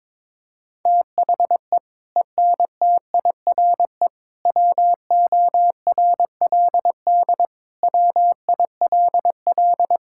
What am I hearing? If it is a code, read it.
Morse code, heard as a keyed tone, about 22 wpm, 700 Hz. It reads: THE ENTIRE WORLD WILL
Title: Sentences from Top 1000 Words - Review Part 5 - 22wpm